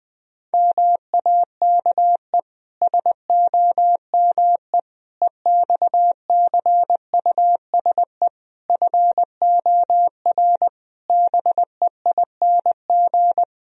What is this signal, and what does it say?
Morse code, heard as a keyed tone, about 20 wpm, 700 Hz. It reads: MAKE SOME EXCUSE FOR BEING
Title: Sentences from Top 1000 Words - Review Part 3 - 20wpm